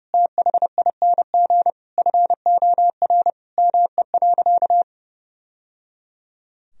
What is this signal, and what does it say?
Morse code, heard as a keyed tone, about 30 wpm, 700 Hz. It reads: THING FOR ME.
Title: Sentences from Top 1000 Words - Review Part 4 - 30wpm